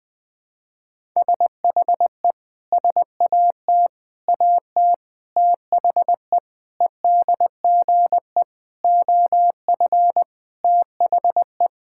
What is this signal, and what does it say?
Morse code, heard as a keyed tone, about 20 wpm, 700 Hz. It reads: SHE SAT AT THE EDGE OF THE
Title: Sentences with 4-Word Phrases - Spoken Beginning - 20wpm